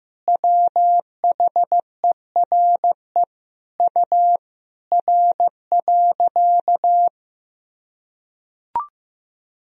Morse code, heard as a keyed tone, about 16 wpm, 700 Hz. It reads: WHERE U R.
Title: Sentences from Top 500 Words and Abbreviations 15wpm